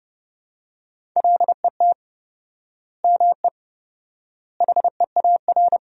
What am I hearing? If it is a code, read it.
Morse code, heard as a keyed tone, about 30 wpm, 700 Hz. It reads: LET ME HEAR
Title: 4x Word-Spacing - Sentences from Top 400 Words - 30wpm